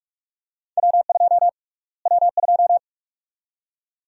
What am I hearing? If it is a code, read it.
Morse code, heard as a keyed tone, about 46 wpm, 700 Hz. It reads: W2 W2
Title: USA Call Sign Prefixes - Repeated 2x - 45wpm